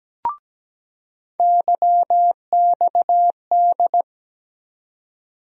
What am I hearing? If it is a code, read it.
Morse code, heard as a keyed tone, about 17 wpm, 700 Hz. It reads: YXD
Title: USA Call Sign Postfixes - 3 Letters 17wpm